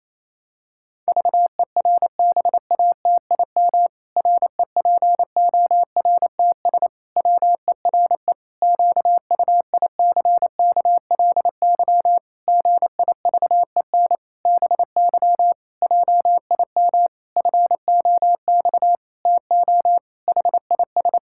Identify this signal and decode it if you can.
Morse code, heard as a keyed tone, about 28 wpm, 700 Hz. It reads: VERBATIM REPORTS WERE QUICKLY GIVEN BY JIM FOX TO HIS
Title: Pangram Challenge v2 - 28wpm